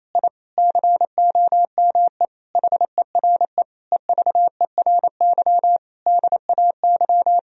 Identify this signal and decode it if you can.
Morse code, heard as a keyed tone, about 28 wpm, 700 Hz. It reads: I COME HERE EVERY DAY
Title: Sentences from Top 1000 Words - Review Part 2 - 28wpm